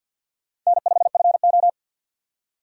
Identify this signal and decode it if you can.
Morse code, heard as a keyed tone, about 52 wpm, 700 Hz. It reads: N5FO
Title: USA Call Signs - 4 Characters - 50wpm